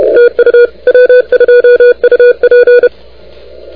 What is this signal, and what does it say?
Morse code, heard as a keyed tone, about 30 wpm, 500 Hz. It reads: TU W2UP
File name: recordings / W2UP de NT1K ARRL10M CW 28MHz